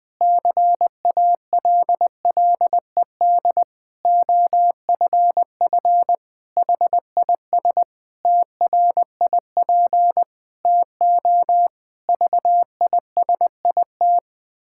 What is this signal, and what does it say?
Morse code, heard as a keyed tone, about 21 wpm, 700 Hz. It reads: CALLED OFF HIS TRIP TO VISIT
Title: Pangram Challenge v2 - 20wpm